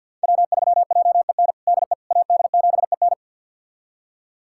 Morse code, heard as a keyed tone, about 52 wpm, 700 Hz. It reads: W3JEN DE AD7EN
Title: USA Callsign DE Callsign - 5 Characters - 50wpm